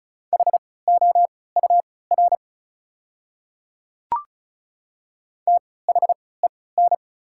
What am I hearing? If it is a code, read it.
Morse code, heard as a keyed tone, about 34 wpm, spaced as for 20 wpm, 690 Hz. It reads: HOUR THEN
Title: 3x Character Spacing - Top 500 Words - 35wpm